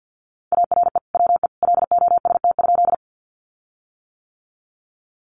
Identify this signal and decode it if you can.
Morse code, heard as a keyed tone, about 50 wpm, 700 Hz. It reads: ARE WE LOST?
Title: Sentences from Top 1400 Words 50wpm